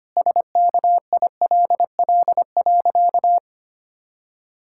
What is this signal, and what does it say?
Morse code, heard as a keyed tone, about 25 wpm, 700 Hz. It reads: SKILL.
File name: practice / 25wpm / Sentences with 3-Word Phrases - 25wpm